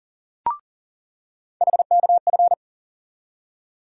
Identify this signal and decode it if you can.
Morse code, heard as a keyed tone, about 40 wpm, 700 Hz. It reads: HKF
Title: USA Call Sign Postfixes - 3 Letters 40wpm